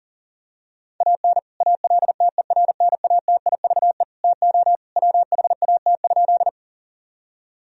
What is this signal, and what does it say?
Morse code, heard as a keyed tone, about 40 wpm, 700 Hz. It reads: AN ALTERNATIVE TO WHAT?
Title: Sentences from Top 1000 Words - Review Part 4 - 40wpm